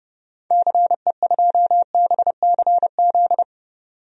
Morse code, heard as a keyed tone, about 31 wpm, 700 Hz. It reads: CE2BCZ
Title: DXCC-100 Call Signs - 6 Characters - 30wpm